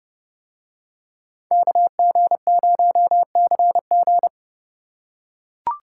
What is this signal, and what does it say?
Morse code, heard as a keyed tone, about 30 wpm, 700 Hz. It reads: KG0CG